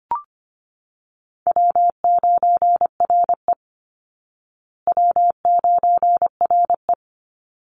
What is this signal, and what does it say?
Morse code, heard as a keyed tone, about 25 wpm, 700 Hz. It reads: W9RE W9RE
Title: Easy Call Signs Repeated 2x - 25wpm